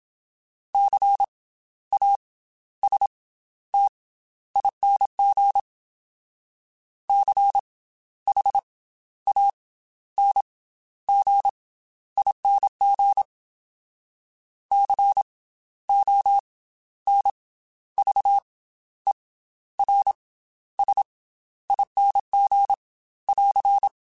Morse code, heard as a keyed tone, about 27 wpm, 800 Hz. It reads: C A S T ING C H A N G ING C O N V E R S ING <AR>
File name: Special INGs / Special INGs-mp3s / Slow ing @25